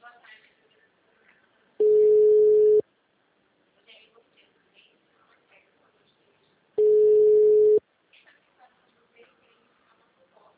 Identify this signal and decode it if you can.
Ringback tone (Europe, 425 Hz), with speech.
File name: out- -2012- - - .1